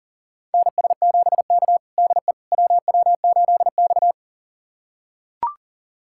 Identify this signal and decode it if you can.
Morse code, heard as a keyed tone, about 40 wpm, 700 Hz. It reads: NS7K DE WW8X
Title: USA Callsign DE Callsign - 4 Characters - 40wpm